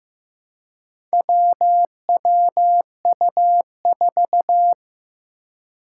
Morse code, heard as a keyed tone, about 15 wpm, 700 Hz. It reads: WWU4
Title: Four Challenging Letter-Number Combinations 15wpm